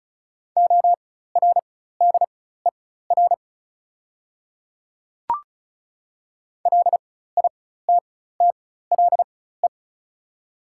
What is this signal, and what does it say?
Morse code, heard as a keyed tone, about 34 wpm, spaced as for 16 wpm, 690 Hz. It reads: ORDER LITTLE
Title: 4x Character Spacing - Top 500 Words - 35wpm